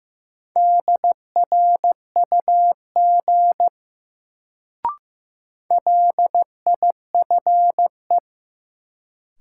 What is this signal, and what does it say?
Morse code, heard as a keyed tone, about 15 wpm, 700 Hz. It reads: DRUG LIFE
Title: Sets of 5 Words - New Word Drill - 15wpm